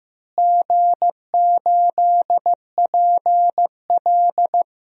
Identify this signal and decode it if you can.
Morse code, heard as a keyed tone, about 15 wpm, 700 Hz. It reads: G8PL
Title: DXCC-100 Call Signs - 4 Characters - Repeated 3x - 15wpm